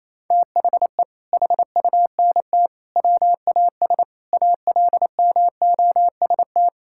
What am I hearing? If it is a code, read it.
Morse code, heard as a keyed tone, about 28 wpm, 700 Hz. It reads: THE HUNT WAS ALMOST
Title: Sentences from Top 1600 Words 28wpm